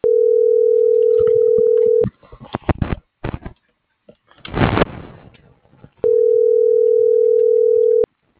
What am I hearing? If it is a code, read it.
Ringback tone (North America), 440 + 480 Hz.